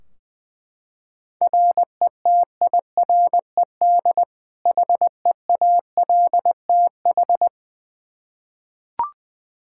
Morse code, heard as a keyed tone, about 20 wpm, 700 Hz. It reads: RETIRED HEALTH
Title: Practice QSO - Exchange 3 - More Info - 20wpm